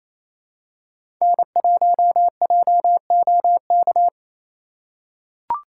Morse code, heard as a keyed tone, about 28 wpm, 700 Hz. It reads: N1JOK
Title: USA Call Signs - 5 Characters - 4x Character Spacing - 28wpm